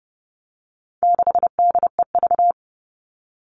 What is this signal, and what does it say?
Morse code, heard as a keyed tone, about 30 wpm, 700 Hz. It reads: 6DEV
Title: All Cut Numbers - 4-Digit 30wpm